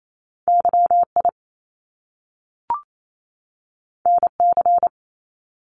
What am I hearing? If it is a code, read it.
Morse code, heard as a keyed tone, about 28 wpm, 700 Hz. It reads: Y I N C
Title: Top Two Letter Combinations in Words 28wpm